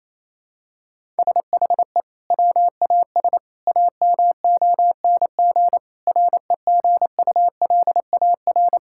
Morse code, heard as a keyed tone, about 28 wpm, 700 Hz. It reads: SHE WAS AMONG REGULAR
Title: Sentences from Top 1000 Words - Review Part 6 - 28wpm